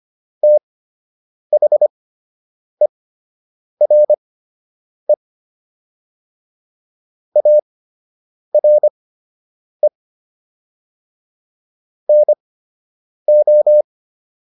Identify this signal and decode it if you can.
Morse code, heard as a keyed tone, about 25 wpm, spaced as for 8 wpm, 600 Hz. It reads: THERE ARE NO